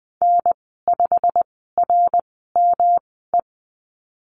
Morse code, heard as a keyed tone, about 20 wpm, spaced as for 14 wpm, 700 Hz. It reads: N5RME
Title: 140 Lesson 23g - Letter f - Call Signs 20wpm